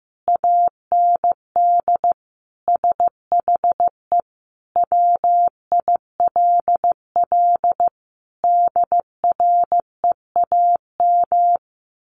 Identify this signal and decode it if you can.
Morse code, heard as a keyed tone, about 15 wpm, 700 Hz. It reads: AND SHE WILL DREAM
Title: Sentences from Top 1500 Words 15wpm